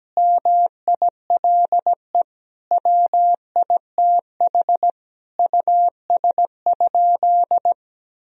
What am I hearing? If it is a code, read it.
Morse code, heard as a keyed tone, about 17 wpm, 700 Hz. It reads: MILE WITH US?
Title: Sentences from Top 1000 Words - Review Part 1 - 17wpm